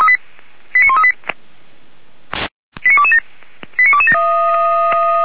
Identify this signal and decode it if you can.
Five-tone selective call (ZVEI), heard as quick runs of five tones, tones of about 70 ms. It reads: 89138 89227 89227